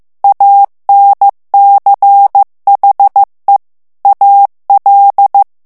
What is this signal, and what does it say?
Morse code, heard as a keyed tone, about 15 wpm, 800 Hz. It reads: ANCHE AL
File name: morse mp3 / ITALIEN texte clair 2 vitesse 14 mots minute